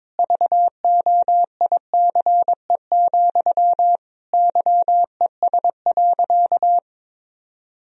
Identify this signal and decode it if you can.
Morse code, heard as a keyed tone, about 22 wpm, 700 Hz. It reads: VOICE, YES.
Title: Sentences from Top 1000 Words - Review Part 1 - 22wpm